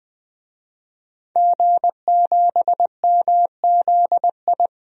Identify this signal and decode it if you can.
Morse code, heard as a keyed tone, about 20 wpm, 700 Hz. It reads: G7MZI
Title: DXCC-100 Call Signs - 5 Characters - 20wpm